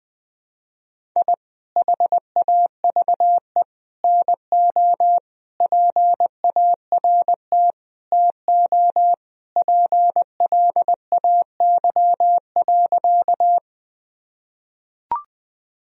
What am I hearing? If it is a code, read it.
Morse code, heard as a keyed tone, about 20 wpm, 700 Hz. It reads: I HAVE NO PART TO PLAY.
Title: Sentences from Top 1000 Words - Review Part 1 - 20wpm